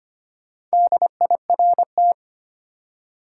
Morse code, heard as a keyed tone, about 25 wpm, 700 Hz. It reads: DIRT
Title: Four Letter Words - 25wpm